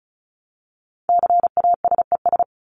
Morse code, heard as a keyed tone, about 34 wpm, 700 Hz. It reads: CASES
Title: ES:S Suffix - Top 100 Words with Base Words 35wpm